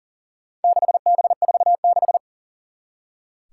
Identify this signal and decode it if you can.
Morse code, heard as a keyed tone, about 42 wpm, 700 Hz. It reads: 6B46
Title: Four Challenging Letter-Number Combinations 40wpm